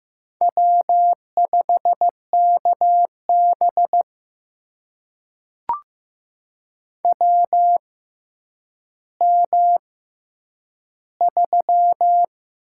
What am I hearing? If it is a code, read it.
Morse code, heard as a keyed tone, about 15 wpm, spaced as for 9 wpm, 700 Hz. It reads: W5KB WM3
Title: USA Call Signs - 4 Characters - 6x Character Spacing - 15wpm